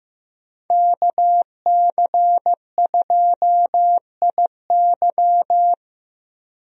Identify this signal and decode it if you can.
Morse code, heard as a keyed tone, about 16 wpm, 700 Hz. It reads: KC2IY
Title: USA Call Signs - 5 Characters - Repeated 3x - 15wpm